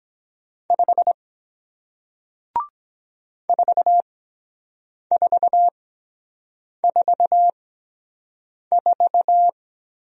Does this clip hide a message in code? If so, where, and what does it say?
Yes: Morse code from 0.7 to 9.5 s, heard as a keyed tone, about 23 wpm, spaced as for 7 wpm, 700 Hz. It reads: 5 4444